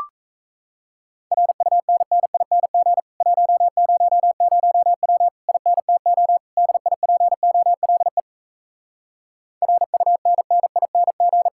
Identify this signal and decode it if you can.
Morse code, heard as a keyed tone, about 38 wpm, 700 Hz. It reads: RUNNING 100W INTO DIPOLE RUNNING